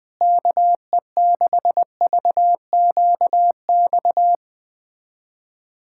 Morse code, heard as a keyed tone, about 21 wpm, 700 Hz. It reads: KE6VQX
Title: USA Callsign DE Callsign 20wpm